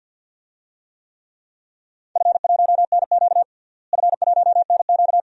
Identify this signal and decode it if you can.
Morse code, heard as a keyed tone, about 48 wpm, 700 Hz. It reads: F1NQ F1NQ
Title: DXCC-100 Call Signs - 4 Characters - Repeated 2x - 50wpm